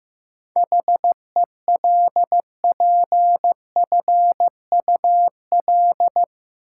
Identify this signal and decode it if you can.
Morse code, heard as a keyed tone, about 15 wpm, 700 Hz. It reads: HELPFUL
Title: Sentences from Top 1000 Words - Review Part 6 - New Word Drill - 15wpm